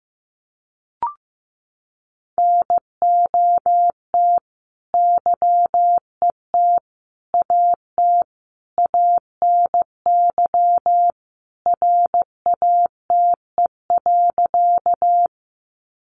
Morse code, heard as a keyed tone, about 15 wpm, 700 Hz. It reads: NOT YET AT ANY RATE.
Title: Sentences from Top 700 Words 15wpm